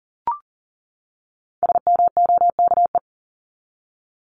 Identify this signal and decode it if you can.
Morse code, heard as a keyed tone, about 40 wpm, 700 Hz. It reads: SMOKE
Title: Top 1501-1750 Words 40wpm